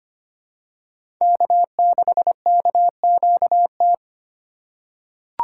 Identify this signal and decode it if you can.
Morse code, heard as a keyed tone, about 25 wpm, 690 Hz. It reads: K6KQT
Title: USA Call Signs - 5 Characters - 6x Character Spacing - 25wpm